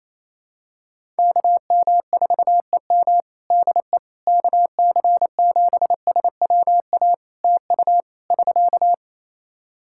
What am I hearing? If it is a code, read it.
Morse code, heard as a keyed tone, about 28 wpm, 700 Hz. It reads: KM4EM DE KC7SWA TU <SK>
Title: Practice QSO - Exchange 4 - Ending - 28wpm